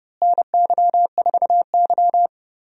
Morse code, heard as a keyed tone, about 30 wpm, 700 Hz. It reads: NY4Y
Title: USA Call Signs - 4 Characters - Repeated 2x - 30wpm